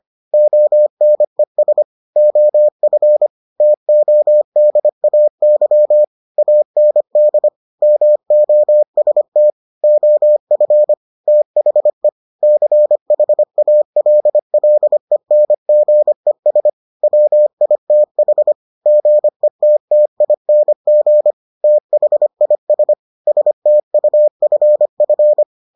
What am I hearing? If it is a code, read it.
Morse code, heard as a keyed tone, about 25 wpm, 600 Hz. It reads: ONES OF TODAY AND MOST OF THE CHALLENGES WITH GETTING THIS STUFF